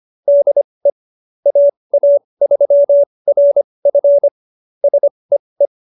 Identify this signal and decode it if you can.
Morse code, heard as a keyed tone, about 25 wpm, spaced as for 20 wpm, 570 Hz. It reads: DE AA3RF SEE